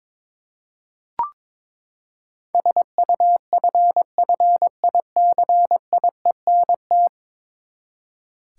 Morse code, heard as a keyed tone, about 21 wpm, 700 Hz. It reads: SUFFICIENT
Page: morse-code-ninja-practice-22wpm